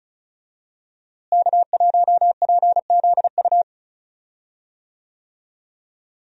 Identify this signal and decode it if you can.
Morse code, heard as a keyed tone, about 36 wpm, 700 Hz. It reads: K1PZU
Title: USA Call Signs - 5 Characters - Repeated 2x - 35wpm